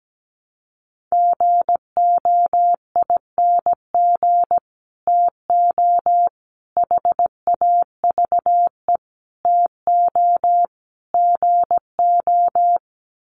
Morse code, heard as a keyed tone, about 17 wpm, 700 Hz. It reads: GOING TO HAVE TO GO
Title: Sets of 5 Words Encore 17wpm